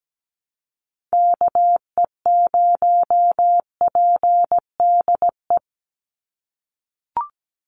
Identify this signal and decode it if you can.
Morse code, heard as a keyed tone, about 17 wpm, 700 Hz. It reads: KE0PDE
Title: USA Call Signs 17wpm